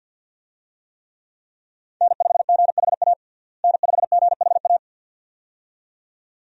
Morse code, heard as a keyed tone, about 52 wpm, 700 Hz. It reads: N5GHA N5GHA
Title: USA Call Signs - 5 Characters - Repeated 2x - 50wpm